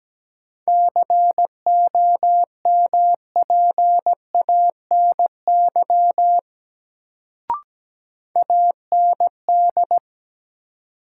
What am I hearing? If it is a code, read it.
Morse code, heard as a keyed tone, about 17 wpm, 700 Hz. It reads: COMPANY AND
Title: Sets of 2 Words - New Word Drill - 17wpm